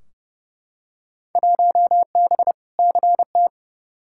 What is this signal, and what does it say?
Morse code, heard as a keyed tone, about 29 wpm, 700 Hz. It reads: 1B CT
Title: Field Day Exchanges Encore - 30wpm